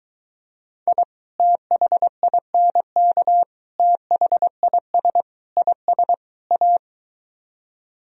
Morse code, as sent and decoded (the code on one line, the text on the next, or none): .. / - .... .. -. -.- / - .... .. ... / .. ... / .-
I THINK THIS IS A